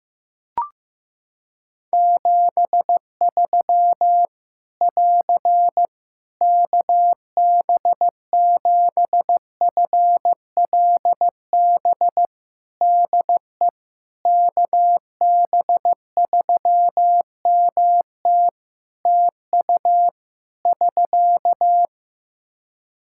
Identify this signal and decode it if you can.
Morse code, heard as a keyed tone, about 15 wpm, 700 Hz. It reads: 73 <AR> KB7FLB DE KB3MT TU <SK>
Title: Practice QSO - Exchange 4 - Ending - 15wpm